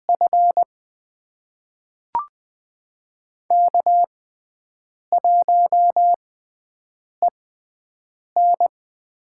Morse code, heard as a keyed tone, about 20 wpm, spaced as for 7 wpm, 700 Hz. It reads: F K1EN
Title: USA Call Signs - 4 Characters - 6x Character Spacing - 20wpm